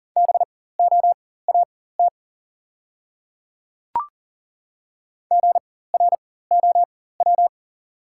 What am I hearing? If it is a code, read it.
Morse code, heard as a keyed tone, about 40 wpm, spaced as for 19 wpm, 690 Hz. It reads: BOAT GROW
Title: 4x Character Spacing - Top 500 Words - 40wpm